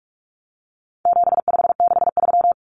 Morse code, heard as a keyed tone, about 44 wpm, 700 Hz. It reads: 7563
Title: Numbers - 4-Digit 45wpm